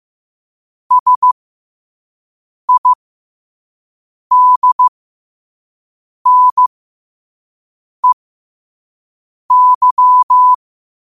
Morse code, heard as a keyed tone, about 15 wpm, spaced as for 5 wpm, 1000 Hz. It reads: SIDNEY
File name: qotdfiles / 5wpm